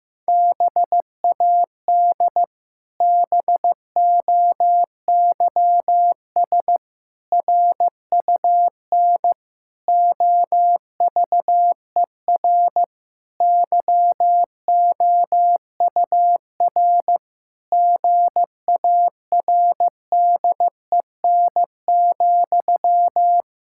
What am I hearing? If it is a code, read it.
Morse code, heard as a keyed tone, about 15 wpm, 700 Hz. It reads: BAD BOYS RUN OVER YOUR GARDEN,
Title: Pangram Challenge v2 - 15wpm